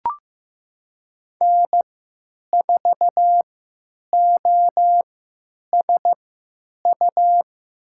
Morse code, heard as a keyed tone, about 15 wpm, spaced as for 9 wpm, 700 Hz. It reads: N4OSU
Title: USA Call Signs - 5 Characters - 3x Character Spacing - 15wpm